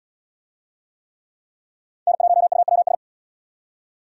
Morse code, heard as a keyed tone, about 64 wpm, 700 Hz. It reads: N1UWS